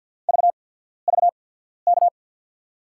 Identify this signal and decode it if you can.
Morse code, heard as a keyed tone, about 53 wpm, spaced as for 14 wpm, 700 Hz. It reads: VVK
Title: USA Call Signs - 5 Characters - 8x Character Spacing - 50wpm